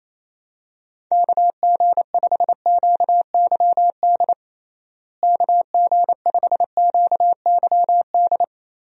Morse code, heard as a keyed tone, about 28 wpm, 700 Hz. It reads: KG5QYD KG5QYD
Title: USA Call Signs - 6 Characters - Repeated 3x - 28wpm